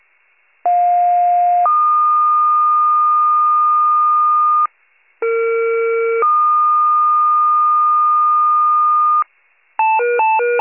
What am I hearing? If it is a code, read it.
Two-tone page, heard as a one-second tone, then a three-second tone, about 710 then 1220 Hz, about 460 then 1220 Hz. It is heard from 0.6 to 9.2 s.
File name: pager / 21 RESCUE / wcalert